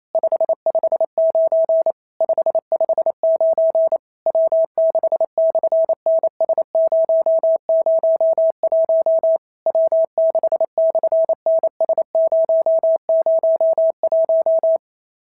Morse code, heard as a keyed tone, about 28 wpm, 650 Hz. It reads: 559 559 W6/NS001 W6/NS001
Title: SOTA QSO Practice 28wpm